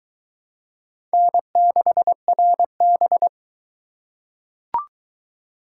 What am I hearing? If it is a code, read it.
Morse code, heard as a keyed tone, about 20 wpm, 700 Hz. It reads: N6RB